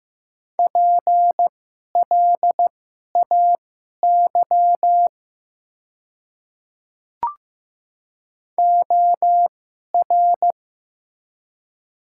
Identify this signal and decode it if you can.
Morse code, heard as a keyed tone, about 15 wpm, spaced as for 11 wpm, 700 Hz. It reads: PLAY OR
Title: 2x Character Spacing - Top 300 Words - 15wpm